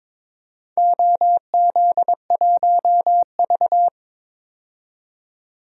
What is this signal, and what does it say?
Morse code, heard as a keyed tone, about 22 wpm, 700 Hz. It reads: OZ1V
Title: DXCC-100 Call Signs - 4 Characters - 22wpm